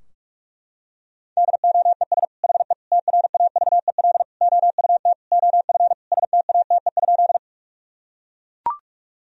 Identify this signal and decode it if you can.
Morse code, heard as a keyed tone, about 41 wpm, 700 Hz. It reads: DOES HE TRAVEL OUT OF STATE?
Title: Sentences from Top 1000 Words - Review Part 2 - 45wpm